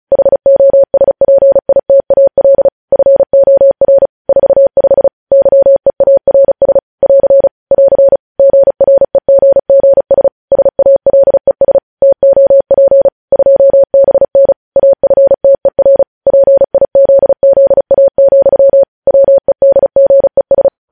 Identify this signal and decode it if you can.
Morse code, heard as a keyed tone, about 36 wpm, 550 Hz. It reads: HOSPITAL FOR 45 YEARS <AR> <AR> GREGGS SALES TOP 2BN AFTER PIZZA, WEDGES
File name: podcast / News headlines-1x-35x35